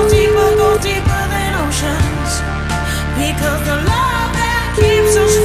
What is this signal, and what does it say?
Ringback tone (Europe, 425 Hz), with noise.